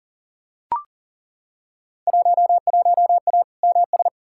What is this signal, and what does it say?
Morse code, heard as a keyed tone, about 37 wpm, 700 Hz. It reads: 11A MS
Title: Field Day Exchanges Encore - 40wpm